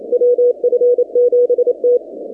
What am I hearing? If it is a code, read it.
Morse code, heard as a keyed tone, about 27 wpm, 490 Hz. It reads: WF7T
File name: 2019-cw-wpx-kt5j-k5pi